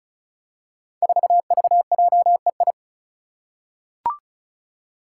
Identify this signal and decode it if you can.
Morse code, heard as a keyed tone, about 36 wpm, 700 Hz. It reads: 4VJEI
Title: Five Challenging Letter-Number Combinations 35wpm